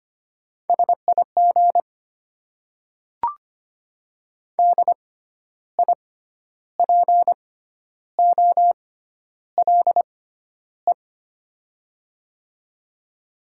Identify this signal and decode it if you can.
Morse code, heard as a keyed tone, about 25 wpm, spaced as for 11 wpm, 700 Hz. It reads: SIG DIPOLE
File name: Top 90 QSO Elements - 6x Character Spacing - 25wpm